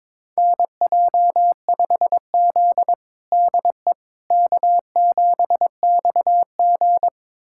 Morse code, heard as a keyed tone, about 21 wpm, 700 Hz. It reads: NJ5Z DE K7XG
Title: USA Callsign DE Callsign - 4 Characters - 22wpm